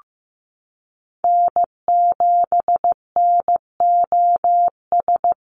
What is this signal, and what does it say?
Morse code, heard as a keyed tone, about 15 wpm, 700 Hz. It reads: N7NOS
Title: USA Call Signs 15wpm